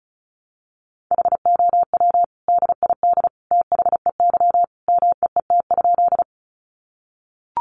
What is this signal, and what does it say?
Morse code, heard as a keyed tone, about 34 wpm, 700 Hz. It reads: HOW DID THEY MEET?
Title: Sentences from Top 1100 Words 35wpm